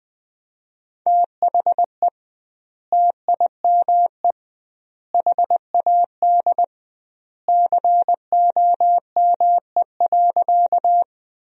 Morse code, heard as a keyed tone, about 21 wpm, 700 Hz. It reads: THE TIME HAD COME.
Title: 2x Word-Spacing - Sentences from Top 100 Words - 20wpm